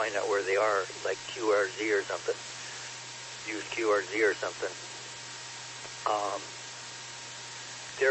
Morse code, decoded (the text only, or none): none